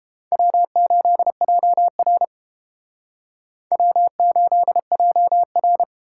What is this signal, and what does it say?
Morse code, heard as a keyed tone, about 31 wpm, 700 Hz. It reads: W8JR W8JR